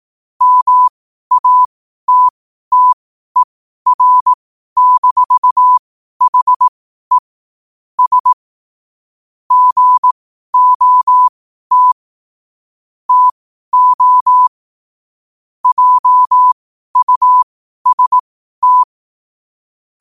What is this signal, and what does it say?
Morse code, heard as a keyed tone, about 18 wpm, spaced as for 12 wpm, 1000 Hz. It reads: MATTER-HE S GOT TO JUST
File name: qotdfiles / 12wpm